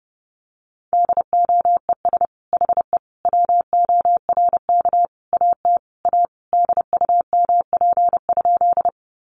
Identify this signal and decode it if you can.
Morse code, heard as a keyed tone, about 28 wpm, 700 Hz. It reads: DOES HE WORK AT A DUMP?
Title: Sentences from Top 1300 Words 30wpm